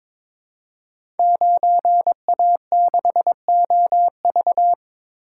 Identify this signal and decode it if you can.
Morse code, heard as a keyed tone, about 22 wpm, 690 Hz. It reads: 9A6OV
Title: Code Groups - 5 Letter:Numbers - 22wpm